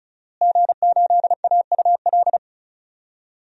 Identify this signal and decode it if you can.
Morse code, heard as a keyed tone, about 34 wpm, 700 Hz. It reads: G8AUL
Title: DXCC-100 Call Signs - 5 Characters - Repeated 2x - 35wpm